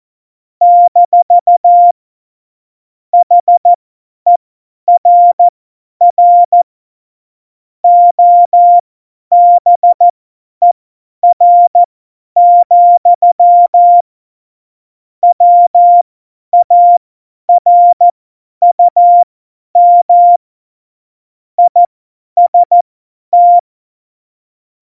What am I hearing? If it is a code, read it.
Morse code, heard as a keyed tone, about 14 wpm, spaced as for 10 wpm, 700 Hz. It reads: - HERR OBER, WARUM IST
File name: Morsekurs / Witze / Witz 10.Dezember 2019, 50 F70